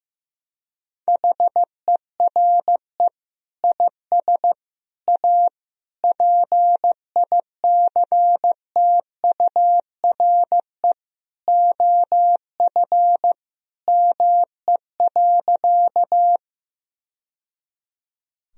Morse code, heard as a keyed tone, about 16 wpm, 700 Hz. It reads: HERE IS A PICTURE OF ME.
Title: Sentences from Top 1000 Words - Review Part 1 - 15wpm